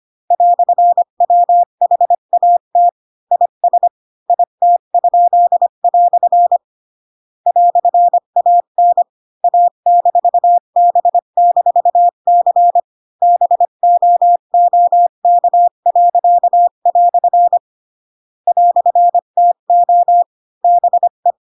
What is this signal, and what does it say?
Morse code, heard as a keyed tone, about 25 wpm, 700 Hz. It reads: "WHAT IS IT?" "AN A-B-C BOOK." "TO BE
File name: Pinocchio / Pinocchio ch8 20 25 ik1hge